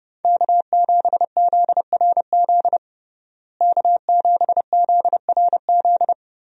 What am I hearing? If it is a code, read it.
Morse code, heard as a keyed tone, about 30 wpm, 700 Hz. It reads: K7ZRZ K7ZRZ
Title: USA Call Signs - 5 Characters - Repeated 2x - 30wpm